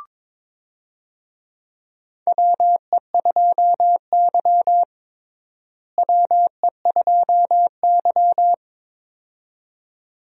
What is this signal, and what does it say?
Morse code, heard as a keyed tone, about 22 wpm, 700 Hz. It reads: WE2Y WE2Y
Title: USA Call Signs - 4 Characters - Repeated 2x - 22wpm